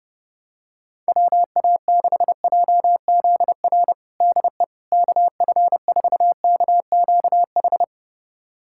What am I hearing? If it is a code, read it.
Morse code, heard as a keyed tone, about 30 wpm, 700 Hz. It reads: WA6JZR DE KF4KQH
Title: USA Callsign DE Callsign - 6 Characters - 30wpm